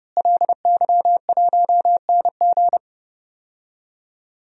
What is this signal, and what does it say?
Morse code, heard as a keyed tone, about 32 wpm, 700 Hz. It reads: LY1NG
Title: DXCC-100 Call Signs - 5 Characters - Repeated 3x - 30wpm